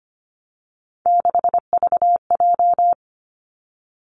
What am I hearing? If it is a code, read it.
Morse code, heard as a keyed tone, about 25 wpm, 700 Hz. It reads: 6VJ
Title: Three Challenging Letter-Number Combinations 25wpm